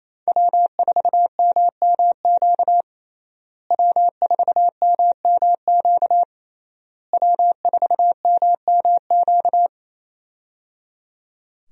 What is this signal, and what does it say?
Morse code, heard as a keyed tone, about 28 wpm, 700 Hz. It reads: W4MMQ W4MMQ W4MMQ
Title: USA Call Signs - 5 Characters - Repeated 3x - 28wpm